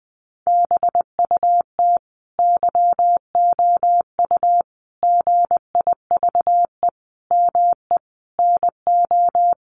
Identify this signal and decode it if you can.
Morse code, heard as a keyed tone, about 20 wpm, 700 Hz. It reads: BUT YOU GIVE ME NO
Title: Sentences from Top 1500 Words 20wpm